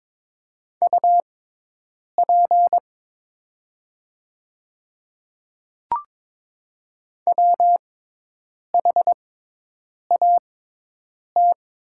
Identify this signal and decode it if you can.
Morse code, heard as a keyed tone, about 23 wpm, spaced as for 8 wpm, 700 Hz. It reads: UP WHAT
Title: 6x Character Spacing - Top 100 Words - 22wpm